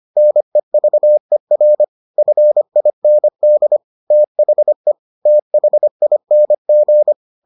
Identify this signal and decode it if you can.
Morse code, heard as a keyed tone, about 25 wpm, 600 Hz. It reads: NEVER FIND THE THING